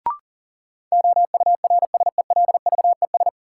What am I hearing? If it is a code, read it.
Morse code, heard as a keyed tone, about 40 wpm, 700 Hz. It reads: OURSELVES
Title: Sentences from Top 1600 Words - New Word Drill - 40wpm